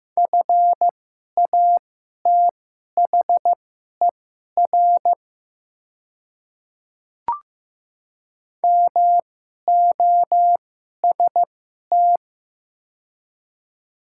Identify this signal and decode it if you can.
Morse code, heard as a keyed tone, about 15 wpm, spaced as for 11 wpm, 700 Hz. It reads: FATHER MOST
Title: 2x Character Spacing - Top 200 Words - 15wpm